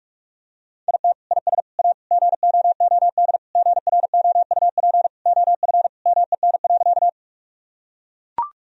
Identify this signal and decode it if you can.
Morse code, heard as a keyed tone, about 42 wpm, 700 Hz. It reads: IT IS A GOOD GROUP OF MEN.
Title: Sentences from Top 1000 Words - Review Part 1 - 45wpm